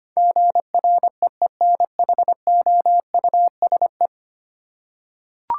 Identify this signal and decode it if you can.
Morse code, heard as a keyed tone, about 25 wpm, 700 Hz. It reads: GREENHOUSE
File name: 10 Letter Words - First 3 Letters Given - 25wpm